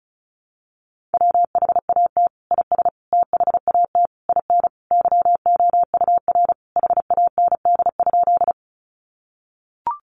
Morse code, heard as a keyed tone, about 34 wpm, 700 Hz. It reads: WHAT IS THAT IN YOUR HAND?
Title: Sentences from Top 200 Words 35wpm